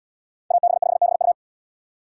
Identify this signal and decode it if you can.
Morse code, heard as a keyed tone, about 72 wpm, 700 Hz. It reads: AB5BU